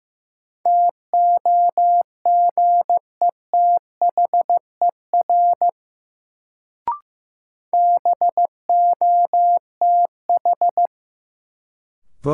Morse code, heard as a keyed tone, about 15 wpm, 700 Hz. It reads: TOGETHER BOTH
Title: Sentences from Top 300 Words - New Word Drill - 15wpm